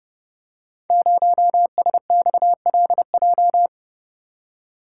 Morse code, heard as a keyed tone, about 30 wpm, 700 Hz. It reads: 0SXLJ
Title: Code Groups - 5 Letter:Numbers - 30wpm